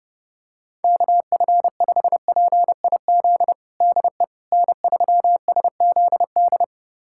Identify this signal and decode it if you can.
Morse code, heard as a keyed tone, about 31 wpm, 700 Hz. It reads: KF5PIZ DE N3SZD
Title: USA Callsign DE Callsign 30wpm